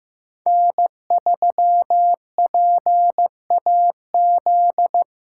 Morse code, heard as a keyed tone, about 15 wpm, 700 Hz. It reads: N3PAZ
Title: USA Call Signs - 3x Character Spacing - 15wpm